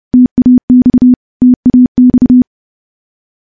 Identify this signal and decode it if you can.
Morse code, heard as a keyed tone, about 30 wpm, 250 Hz. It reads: TAX TAX
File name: morse-training-30WPM